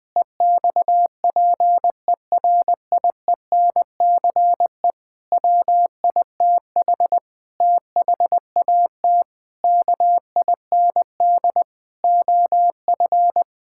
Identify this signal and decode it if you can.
Morse code, heard as a keyed tone, about 19 wpm, 700 Hz. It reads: EXPERIENCE WITH THAT KIND OF
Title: Sentences from Top 1000 Words - Review Part 5 - 20wpm